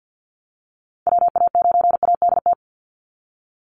Morse code, heard as a keyed tone, about 50 wpm, 700 Hz. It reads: WA9ADT
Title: Call Signs Repeated 3x - 50wpm